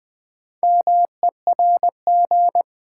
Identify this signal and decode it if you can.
Morse code, heard as a keyed tone, about 20 wpm, 700 Hz. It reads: MERG
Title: Build A Word - 2 Letters At A Time - 6-Letter Words - 20wpm